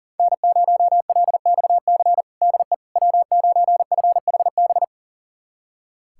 Morse code, heard as a keyed tone, about 40 wpm, 700 Hz. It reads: N0LXC DE W9FHB
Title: USA Callsign DE Callsign - 5 Characters - 40wpm